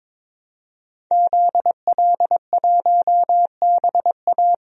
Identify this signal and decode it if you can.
Morse code, heard as a keyed tone, about 22 wpm, 700 Hz. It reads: ZL1BA
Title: DXCC-100 Call Signs - 5 Characters - Repeated 2x - 22wpm